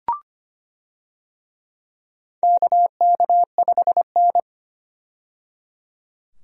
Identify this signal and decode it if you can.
Morse code, heard as a keyed tone, about 25 wpm, 700 Hz. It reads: KK5N
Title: USA Call Signs - 4 Characters - 25wpm